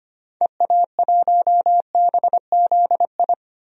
Morse code, heard as a keyed tone, about 25 wpm, 700 Hz. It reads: EA1BZI
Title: DXCC-100 Call Signs - 6 Characters - Repeated 3x - 25wpm